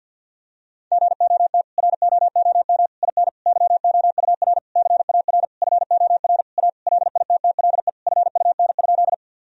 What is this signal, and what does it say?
Morse code, heard as a keyed tone, about 51 wpm, 700 Hz. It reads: GOT ROOM IN YOUR CAR FOR A LITTLE FUN?
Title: Sentences from Top 1000 Words - Review Part 1 - 50wpm